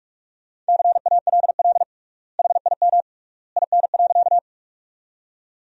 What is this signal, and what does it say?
Morse code, heard as a keyed tone, about 48 wpm, 700 Hz. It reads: CALL HIM IN.
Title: 3x Word-Spacing - Sentences from Top 100 Words - 45wpm